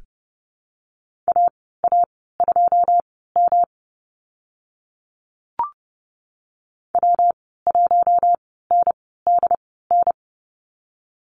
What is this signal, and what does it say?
Morse code, heard as a keyed tone, about 30 wpm, spaced as for 17 wpm, 700 Hz. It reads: AA2M W1NDN
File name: Lesson 19g - Letter w - Call Signs 30wpm